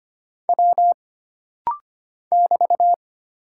Morse code, heard as a keyed tone, about 25 wpm, 700 Hz. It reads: W <BT>
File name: Single Character (Letter:Number:Punctuation:Prosign) - Rapid-Fire - 25wpm